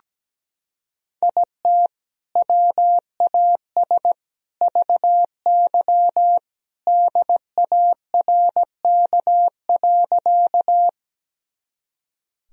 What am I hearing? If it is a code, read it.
Morse code, heard as a keyed tone, about 17 wpm, 700 Hz. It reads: IT WAS VY DARK.
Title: Sentences from Top 500 Words and Abbreviations 17wpm